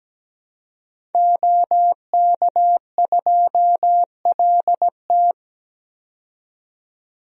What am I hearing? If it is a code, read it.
Morse code, heard as a keyed tone, about 17 wpm, 700 Hz. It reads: OK2LT
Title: DXCC-100 Call Signs - 5 Characters - 17wpm